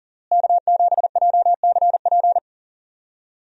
Morse code, heard as a keyed tone, about 40 wpm, 700 Hz. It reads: K7JCP
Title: USA Call Signs - 5 Characters - Repeated 3x - 40wpm